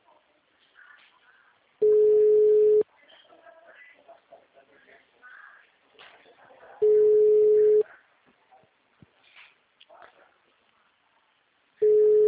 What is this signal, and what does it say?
Ringback tone (Europe, 425 Hz).
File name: out- -2011- - - .81